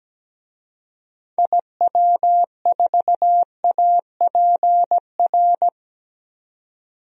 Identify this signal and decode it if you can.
Morse code, heard as a keyed tone, about 17 wpm, 700 Hz. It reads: IW4APR
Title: DXCC-100 Call Signs - 6 Characters - 17wpm